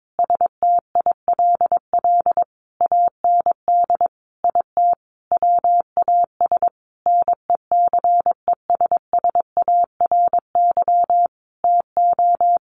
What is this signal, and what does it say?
Morse code, heard as a keyed tone, about 22 wpm, 700 Hz. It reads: STILL AND IT WAS NECESSARY TO
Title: Sentences from Top 800 Words 22wpm